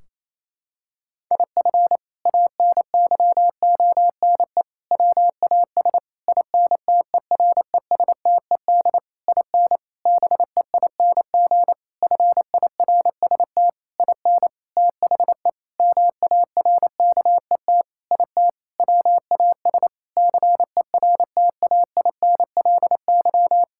Morse code, heard as a keyed tone, about 28 wpm, 700 Hz. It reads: IF ANYONE WAS INTERESTED IN BEING FIRST IN THE MARKET IT WAS CERTAINLY
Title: Sentences from Top 1600 Words 28wpm